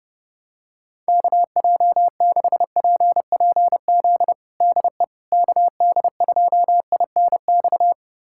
Morse code, heard as a keyed tone, about 30 wpm, 700 Hz. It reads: KJ6PPZ DE KD2INX
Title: USA Callsign DE Callsign 30wpm